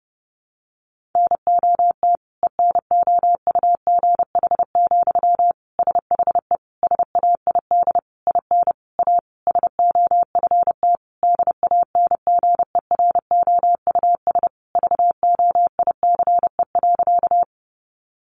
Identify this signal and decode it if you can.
Morse code, heard as a keyed tone, about 30 wpm, 700 Hz. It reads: NOT ENOUGH, SHE SAID IN A SOFT DANGEROUS VOICE.
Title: Sentences from Top 1500 Words 30wpm